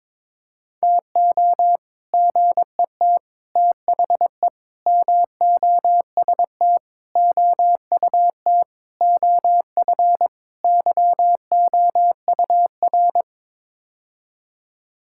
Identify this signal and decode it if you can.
Morse code, heard as a keyed tone, about 22 wpm, 700 Hz. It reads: TO GET THE MOST OUT OF YOUR
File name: practice / 22wpm / Sets of 7 Words - 22wpm